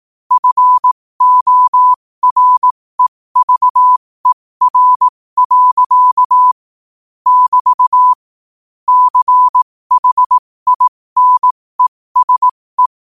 Morse code, heard as a keyed tone, about 18 wpm, spaced as for 15 wpm, 1000 Hz. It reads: FOREVER. <BT> CHINESE
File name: qotdfiles / 15wpm